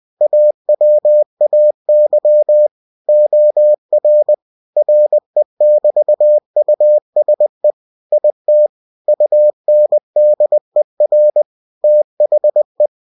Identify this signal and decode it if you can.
Morse code, heard as a keyed tone, about 20 wpm, 600 Hz. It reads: AWAY OR RE<BT>USE IT UNDER THE
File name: winnie-20wpm